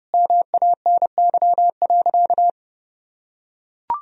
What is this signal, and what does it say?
Morse code, heard as a keyed tone, about 30 wpm, 700 Hz. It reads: MANY.
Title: Sentences from Top 1000 Words - Review Part 1 - 30wpm